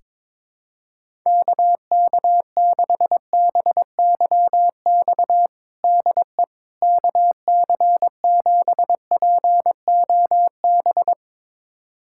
Morse code, heard as a keyed tone, about 22 wpm, 700 Hz. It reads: KK6BYX DE KC7POB
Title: USA Callsign DE Callsign - 6 Characters - 22wpm